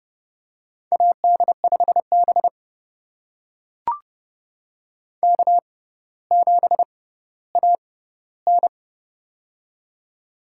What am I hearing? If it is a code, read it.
Morse code, heard as a keyed tone, about 31 wpm, spaced as for 16 wpm, 700 Hz. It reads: AD5B K7AN
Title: USA Call Signs - 4 Characters - 6x Character Spacing - 30wpm